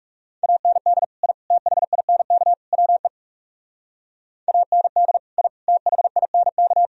Morse code, heard as a keyed tone, about 37 wpm, 700 Hz. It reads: AND I THINK WE AND I THINK
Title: Speed-Racing - Sets of 4 Words 30wpm